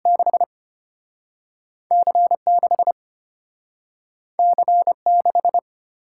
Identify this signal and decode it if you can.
Morse code, heard as a keyed tone, about 33 wpm, 690 Hz. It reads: 6 C6 C6